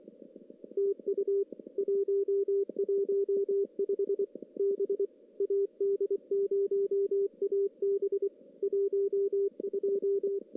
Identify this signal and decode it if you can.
Morse code, heard as a keyed tone, about 26 wpm, 400 Hz. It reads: TU 115 B AD0AB 12
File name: CW / CW-SS-k5tr- -1700